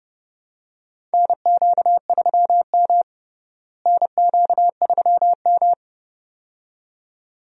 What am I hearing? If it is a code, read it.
Morse code, heard as a keyed tone, about 30 wpm, 700 Hz. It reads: NQ3M NQ3M
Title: USA Call Signs - 4 Characters - Repeated 2x - 30wpm